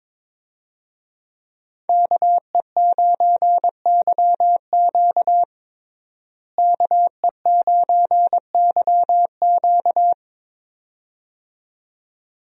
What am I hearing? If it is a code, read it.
Morse code, heard as a keyed tone, about 22 wpm, 700 Hz. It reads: KE9YQ KE9YQ
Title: USA Call Signs - 5 Characters - Repeated 2x - 22wpm